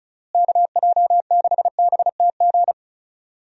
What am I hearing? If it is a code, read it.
Morse code, heard as a keyed tone, about 36 wpm, 700 Hz. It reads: KJ6BTG
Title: USA Call Signs - 6 Characters - Repeated 3x - 35wpm